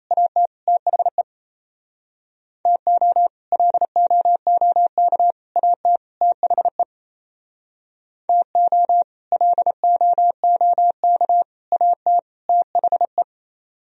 Morse code, heard as a keyed tone, about 36 wpm, 700 Hz. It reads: AT THE TO LOOK AT THE TO LOOK AT THE